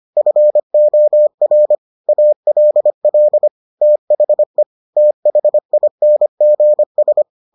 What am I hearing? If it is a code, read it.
Morse code, heard as a keyed tone, about 25 wpm, 600 Hz. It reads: FOR ALL THE THINGS